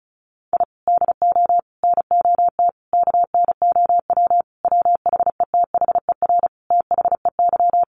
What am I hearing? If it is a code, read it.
Morse code, heard as a keyed tone, about 34 wpm, 700 Hz. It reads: I DO NOT KNOW WHETHER THEY
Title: Sentences from Top 1500 Words 35wpm